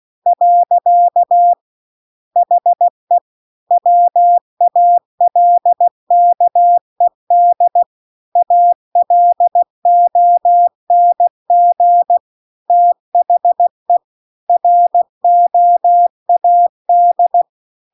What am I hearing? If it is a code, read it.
Morse code, heard as a keyed tone, about 16 wpm, 700 Hz. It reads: . HE WALKED ALONG THE ROAD